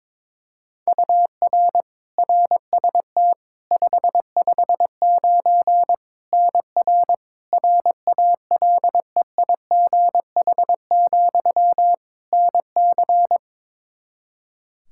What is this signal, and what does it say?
Morse code, heard as a keyed tone, about 22 wpm, 700 Hz. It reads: UR RST 559 NR RALEIGH, NC
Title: Practice QSO - Exchange 1 - Beginning - 22wpm